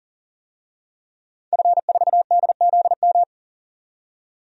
Morse code, heard as a keyed tone, about 39 wpm, 700 Hz. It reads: F4DZM
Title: DXCC-100 Call Signs - 5 Characters - 40wpm